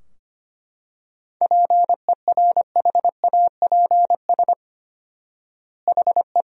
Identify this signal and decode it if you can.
Morse code, heard as a keyed tone, about 25 wpm, 700 Hz. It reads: PERHAPS HE
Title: 4x Word-Spacing - Sentences from Top 500 Words - 25wpm